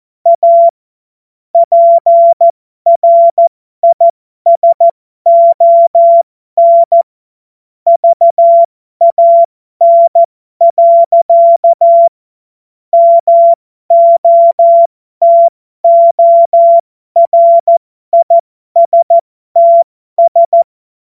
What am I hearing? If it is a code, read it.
Morse code, heard as a keyed tone, about 14 wpm, spaced as for 12 wpm, 680 Hz. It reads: A PRISON VAN. MOTORISTS
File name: Morsekurs / Witze